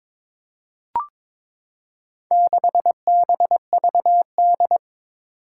Morse code, heard as a keyed tone, about 22 wpm, 700 Hz. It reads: 6BVD
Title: Four Challenging Letter-Number Combinations 22wpm